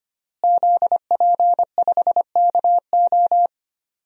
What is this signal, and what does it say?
Morse code, heard as a keyed tone, about 25 wpm, 700 Hz. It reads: ZP5KO
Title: DXCC-100 Call Signs - 5 Characters - Repeated 3x - 25wpm